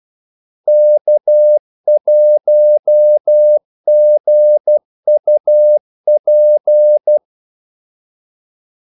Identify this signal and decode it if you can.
Morse code, heard as a keyed tone, about 12 wpm, 600 Hz. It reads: K1GUP
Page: 2 ekstra ordmellemrum mellem kaldesignalerne